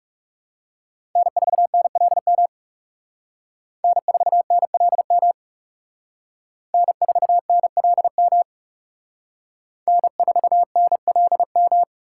Morse code, heard as a keyed tone, about 38 wpm, 700 Hz. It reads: N4NLM N4NLM N4NLM N4NLM